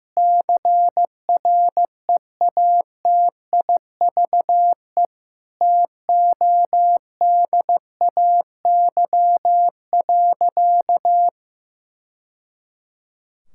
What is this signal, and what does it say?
Morse code, heard as a keyed tone, about 15 wpm, 700 Hz. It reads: CREATIVE TODAY.
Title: Sentences from Top 1000 Words - Review Part 5 - 15wpm